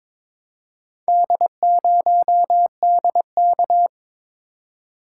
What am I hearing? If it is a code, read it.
Morse code, heard as a keyed tone, about 23 wpm, 700 Hz. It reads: D0DK
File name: practice / 22wpm / Four Letter-Number Combinations - 22wpm